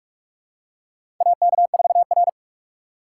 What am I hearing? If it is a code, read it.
Morse code, heard as a keyed tone, about 45 wpm, 700 Hz. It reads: AK4R
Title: USA Call Signs - 8x Character Spacing - 45wpm